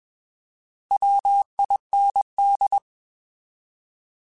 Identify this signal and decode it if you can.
Morse code, heard as a keyed tone, about 21 wpm, 800 Hz. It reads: WIND
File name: Alphabetical / Alphabetical mp3s / Ws @20